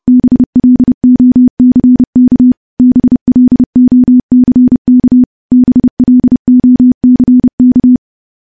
Morse code, heard as a keyed tone, about 32 wpm, 250 Hz. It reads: BLOCK BLOCK BLOCK
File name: morse / morse-training-30WPM